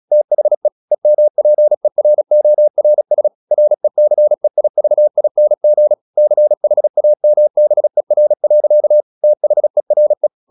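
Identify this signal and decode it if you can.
Morse code, heard as a keyed tone, about 36 wpm, 600 Hz. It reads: THE EMPERORS RECEIVING CHAMBER. THERE
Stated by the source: Morse practice at 36 wpm, text of Ken Follett's novel Never, from CW med Gnister.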